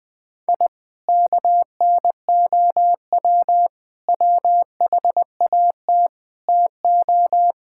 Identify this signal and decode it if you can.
Morse code, heard as a keyed tone, about 20 wpm, 700 Hz. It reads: I KNOW WHAT TO
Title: Sentences with 4-Word Phrases - Spoken Beginning - 20wpm